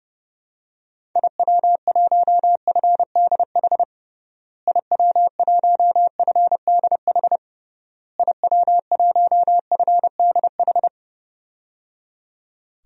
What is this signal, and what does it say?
Morse code, heard as a keyed tone, about 30 wpm, 700 Hz. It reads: IW1FDH IW1FDH IW1FDH
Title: DXCC-100 Call Signs - 6 Characters - Repeated 3x - 30wpm